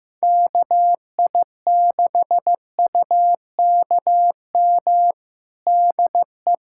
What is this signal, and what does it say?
Morse code, heard as a keyed tone, about 15 wpm, 700 Hz. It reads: KI6UKM DE
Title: USA Callsign DE Callsign - 6 Characters - 15wpm